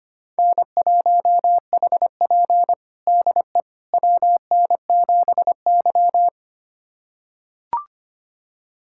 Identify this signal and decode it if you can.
Morse code, heard as a keyed tone, about 25 wpm, 700 Hz. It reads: N1HP DE WN7Y
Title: USA Callsign DE Callsign - 4 Characters - 25wpm